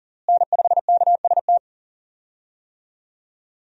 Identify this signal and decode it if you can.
Morse code, heard as a keyed tone, about 40 wpm, 700 Hz. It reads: N5KST
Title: USA Call Signs - 5 Characters - Repeated 2x - 40wpm